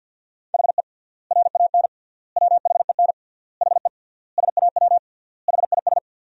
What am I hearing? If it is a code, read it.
Morse code, heard as a keyed tone, about 51 wpm, 700 Hz. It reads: HE RAN WHEN HE SAW HIS
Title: 3x Word-Spacing - Sentences from Top 500 Words - 50wpm